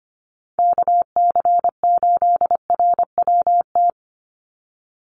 Morse code, heard as a keyed tone, about 25 wpm, 700 Hz. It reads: KC8RWT
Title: USA Call Signs 25wpm